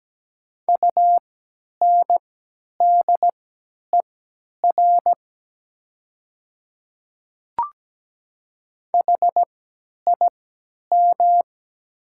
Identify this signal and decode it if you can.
Morse code, heard as a keyed tone, about 17 wpm, spaced as for 10 wpm, 700 Hz. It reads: UNDER HIM
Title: 3x Character Spacing - Top 200 Words - 17wpm